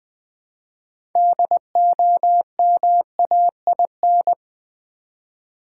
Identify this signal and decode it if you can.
Morse code, heard as a keyed tone, about 20 wpm, 700 Hz. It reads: DOMAIN
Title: Build A Word - 2 Letters At A Time - 6-Letter Words - 20wpm